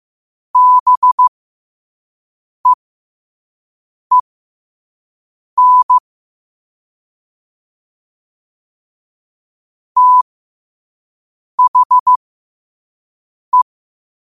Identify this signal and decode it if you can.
Morse code, heard as a keyed tone, about 15 wpm, spaced as for 5 wpm, 1000 Hz. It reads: BEEN THE
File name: qotdfiles / 5wpm